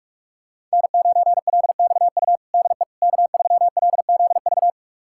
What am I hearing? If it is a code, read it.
Morse code, heard as a keyed tone, about 46 wpm, 700 Hz. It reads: N9LXU DE K3LZV
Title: USA Callsign DE Callsign - 5 Characters - 45wpm